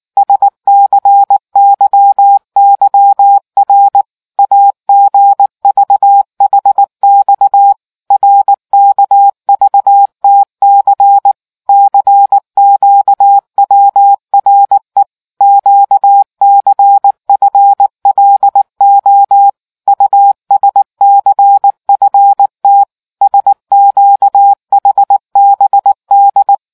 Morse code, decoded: SCYYR AGVHX RKVTC CQWRE QCFLO USCFT SQHBD

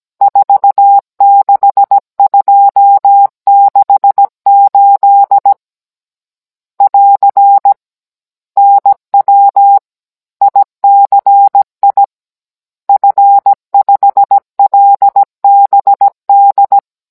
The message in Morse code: 46268 <AR> NW ICI F5LBD

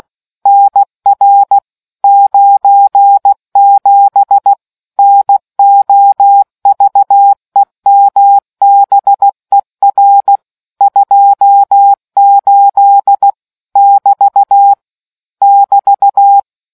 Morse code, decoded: NR 97 NOVEMBER 28 <BT> <BT>

